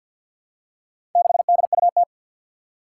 Morse code, heard as a keyed tone, about 46 wpm, 700 Hz. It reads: 6DUT
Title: Four Challenging Letter-Number Combinations 50wpm